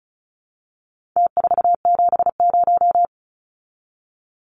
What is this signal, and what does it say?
Morse code, heard as a keyed tone, about 34 wpm, 700 Hz. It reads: T470
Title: Numbers - Contest Style - 4-Digit 35wpm